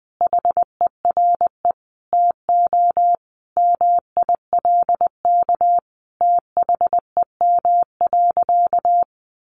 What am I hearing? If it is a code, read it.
Morse code, heard as a keyed tone, about 20 wpm, 700 Hz. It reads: HERE TO MILK THEM.
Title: Sentences from Top 1500 Words 20wpm